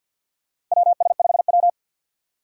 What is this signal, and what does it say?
Morse code, heard as a keyed tone, about 53 wpm, 700 Hz. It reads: WS5W
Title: Four Challenging Letter-Number Combinations 50wpm